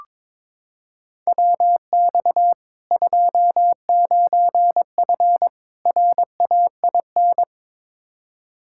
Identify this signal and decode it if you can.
Morse code, heard as a keyed tone, about 22 wpm, 700 Hz. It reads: WX 29F RAIN
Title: K1USN SST QSO Practice 22wpm